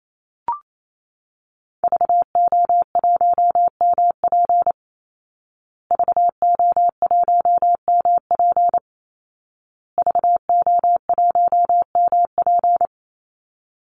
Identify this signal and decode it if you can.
Morse code, heard as a keyed tone, about 28 wpm, 700 Hz. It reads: VO1MP VO1MP VO1MP
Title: Call Signs Repeated 2x - 28wpm